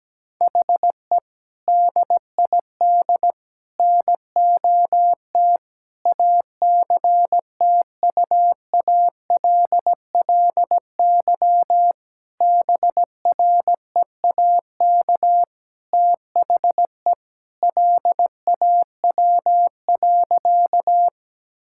Morse code, heard as a keyed tone, about 17 wpm, 700 Hz. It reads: HE DID NOT ACTUALLY BREAK THE LAW.
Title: Sentences from Top 1000 Words - Review Part 6 - 17wpm